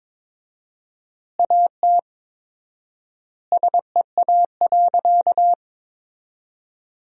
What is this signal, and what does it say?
Morse code, heard as a keyed tone, about 22 wpm, 700 Hz. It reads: AT SEA.
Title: 4x Word-Spacing - Sentences from Top 300 Words - 22wpm